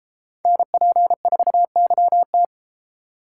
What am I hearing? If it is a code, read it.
Morse code, heard as a keyed tone, about 33 wpm, 700 Hz. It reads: NP4YT